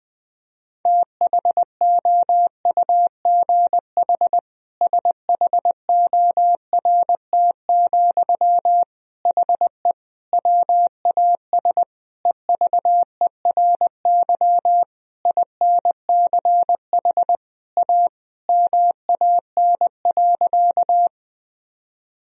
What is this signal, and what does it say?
Morse code, heard as a keyed tone, about 19 wpm, 700 Hz. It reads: THOUGH SHORT, HE WAS EVERY INCH A MAN.
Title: Sentences from Top 1000 Words - Review Part 3 - 20wpm